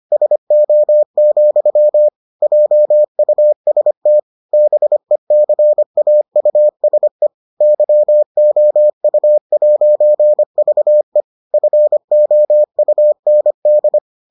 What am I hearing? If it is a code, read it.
Morse code, heard as a keyed tone, about 25 wpm, 600 Hz. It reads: SO, JUST BECAUSE YOU'VE FOUND